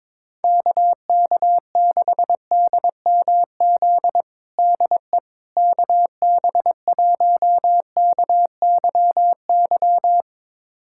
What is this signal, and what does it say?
Morse code, heard as a keyed tone, about 22 wpm, 700 Hz. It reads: KK6DMZ DE KB1KYY
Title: USA Callsign DE Callsign - 6 Characters - 22wpm